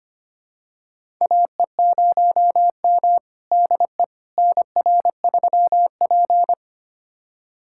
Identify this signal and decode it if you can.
Morse code, heard as a keyed tone, about 25 wpm, 700 Hz. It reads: AE0M DE NR3P
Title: USA Callsign DE Callsign - 4 Characters - 25wpm